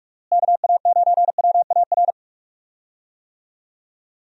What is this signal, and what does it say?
Morse code, heard as a keyed tone, about 46 wpm, 700 Hz. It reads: KA9WAR
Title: USA Call Signs - 6 Characters - Repeated 2x - 45wpm